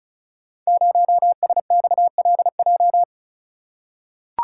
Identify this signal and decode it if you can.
Morse code, heard as a keyed tone, about 36 wpm, 700 Hz. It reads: 0SXLJ
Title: Code Groups - 5 Letter:Numbers - 35wpm